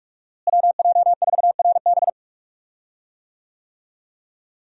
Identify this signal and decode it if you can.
Morse code, heard as a keyed tone, about 42 wpm, 700 Hz. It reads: WJ4RB
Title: USA Call Signs - 5 Characters - Repeated 3x - 45wpm